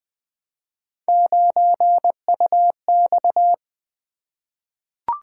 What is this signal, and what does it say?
Morse code, heard as a keyed tone, about 20 wpm, 700 Hz. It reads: 9UX
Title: Three Letter-Number Combinations - 20wpm